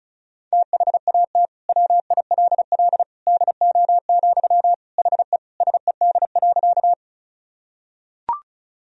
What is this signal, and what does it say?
Morse code, heard as a keyed tone, about 36 wpm, 700 Hz. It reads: THAT WILL DO, HE SED.
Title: Sentences from Top 100 Words and Abbreviations 35wpm